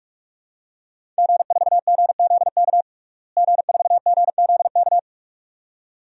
Morse code, heard as a keyed tone, about 42 wpm, 700 Hz. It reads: G4GZK G4GZK
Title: DXCC-100 Call Signs - 5 Characters - Repeated 2x - 45wpm